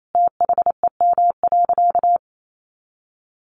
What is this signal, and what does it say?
Morse code, heard as a keyed tone, about 28 wpm, 700 Hz. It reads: THEM.
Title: Sentences from Top 100 Words 28wpm